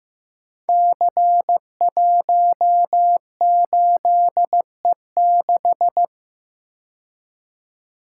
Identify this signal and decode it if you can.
Morse code, heard as a keyed tone, about 15 wpm, 700 Hz. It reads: C18E6
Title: Code Groups - 5 Letter:Numbers - 15wpm